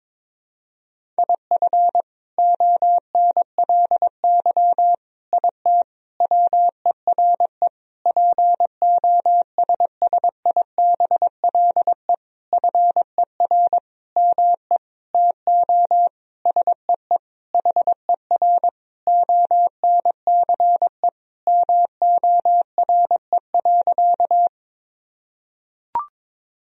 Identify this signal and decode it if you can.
Morse code, heard as a keyed tone, about 22 wpm, 700 Hz. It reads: IF ONLY IT WERE POSSIBLE FER ME TO SEE HER ONCE MORE.
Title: Sentences from Top 500 Words and Abbreviations 22wpm